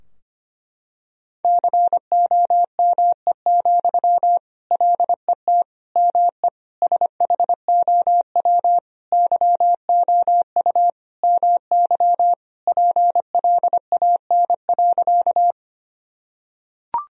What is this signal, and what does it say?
Morse code, heard as a keyed tone, about 25 wpm, 700 Hz. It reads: COME, LET ME SHOW YOU MY PLAN.
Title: Sentences from Top 1000 Words - Review Part 2 - 25wpm